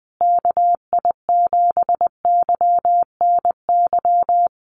Morse code, heard as a keyed tone, about 20 wpm, 700 Hz. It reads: KI7YNY
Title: USA Call Signs Repeated 3x 20wpm